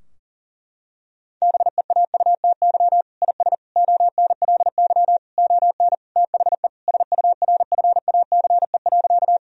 Morse code, heard as a keyed tone, about 40 wpm, 700 Hz. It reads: BEAUTY IS ONLY ON THE SURFACE.
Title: Sentences from Top 1000 Words - Review Part 3 - 40wpm